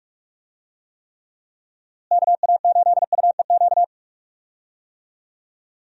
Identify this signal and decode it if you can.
Morse code, heard as a keyed tone, about 45 wpm, 700 Hz. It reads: KA8UEQ
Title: USA Call Signs - 6 Characters - Repeated 3x - 45wpm